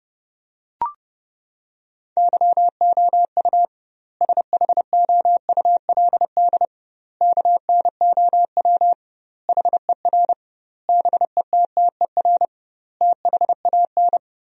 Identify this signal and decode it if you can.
Morse code, heard as a keyed tone, about 30 wpm, 700 Hz. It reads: YOU SHOULD KNOW HER BETTER THAN
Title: 2x Word-Spacing - Sentences from Top 400 Words - 30wpm